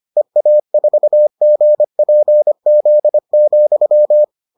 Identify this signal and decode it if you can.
Morse code, heard as a keyed tone, about 25 wpm, 600 Hz. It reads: EA4GPZ,